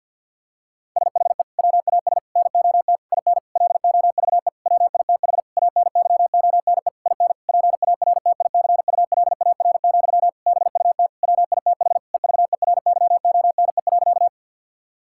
Morse code, heard as a keyed tone, about 50 wpm, 700 Hz. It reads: SHE WAS NOT IN LOVE WITH ANYONE IN PARTICULAR, BUT WITH EVERYONE.
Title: Sentences from Top 1000 Words - Review Part 5 - 50wpm